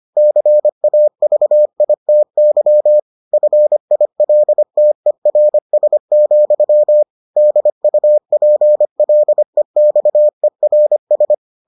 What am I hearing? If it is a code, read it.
Morse code, heard as a keyed tone, about 25 wpm, 600 Hz. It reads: CAVITY FILTERS, DUPLEXERS